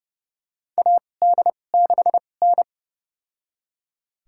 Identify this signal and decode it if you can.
Morse code, heard as a keyed tone, about 29 wpm, spaced as for 21 wpm, 700 Hz. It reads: AD6N
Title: USA Call Signs - 4 Characters - 2x Character Spacing - 30wpm